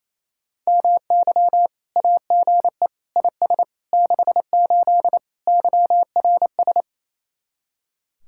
Morse code, heard as a keyed tone, about 36 wpm, 700 Hz. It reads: MY AGE IS 68 YRS